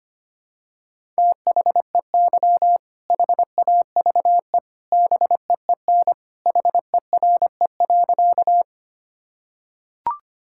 Morse code, heard as a keyed tone, about 25 wpm, 700 Hz. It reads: THEY HAVE BEEN HERE.
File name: practice / 25wpm / Sentences with 2-Word Phrases - 25wpm